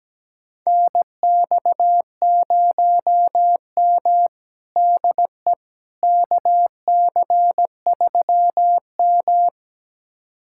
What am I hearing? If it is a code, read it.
Morse code, heard as a keyed tone, about 17 wpm, 700 Hz. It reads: NX0M DE KC3M
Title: USA Callsign DE Callsign - 4 Characters - 17wpm